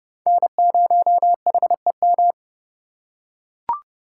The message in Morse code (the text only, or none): N0HEM